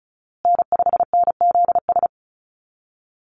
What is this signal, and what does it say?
Morse code, heard as a keyed tone, about 34 wpm, 700 Hz. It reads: N5NZS
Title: USA Call Signs Repeated 3x 35wpm